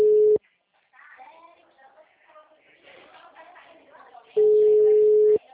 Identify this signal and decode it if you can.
Ringback tone (Europe, 425 Hz), with speech.